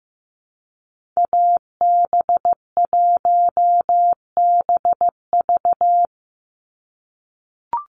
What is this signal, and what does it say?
Morse code, heard as a keyed tone, about 15 wpm, 700 Hz. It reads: AB1BV
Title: USA Call Signs 15wpm